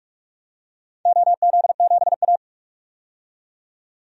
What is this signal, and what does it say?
Morse code, heard as a keyed tone, about 46 wpm, 700 Hz. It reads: OZ7A
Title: DXCC-100 Call Signs - 4 Characters - 45wpm